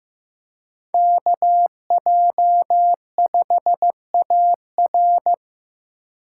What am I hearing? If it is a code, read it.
Morse code, heard as a keyed tone, about 15 wpm, 700 Hz. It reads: KJ5AR
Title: USA Call Signs - 5 Characters - 4x Character Spacing - 15wpm